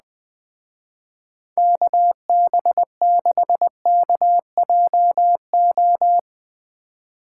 Morse code, heard as a keyed tone, about 19 wpm, 700 Hz. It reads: KB6KJO
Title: Speed Racing - USA 2x3 Call Signs 17wpm